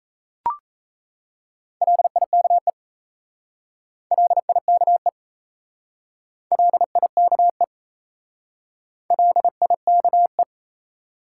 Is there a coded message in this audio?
Yes: Morse code from 1.8 to 10.4 s, heard as a keyed tone, about 37 wpm, 700 Hz. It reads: LIKE LIKE LIKE LIKE